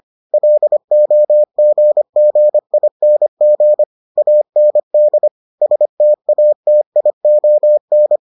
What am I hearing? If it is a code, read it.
Morse code, heard as a keyed tone, about 25 wpm, 600 Hz. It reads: LOGGING AND STATION